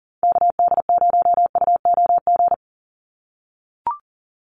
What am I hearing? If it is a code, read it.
Morse code, heard as a keyed tone, about 40 wpm, 700 Hz. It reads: KD0UOG
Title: USA Call Signs 40wpm